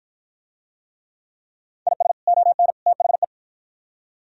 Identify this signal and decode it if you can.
Morse code, heard as a keyed tone, about 48 wpm, 700 Hz. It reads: IS ON THE